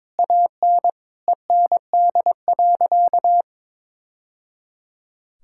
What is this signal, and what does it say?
Morse code, heard as a keyed tone, about 22 wpm, 700 Hz. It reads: AN END.
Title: Sentences from Top 1000 Words - Review Part 6 - 22wpm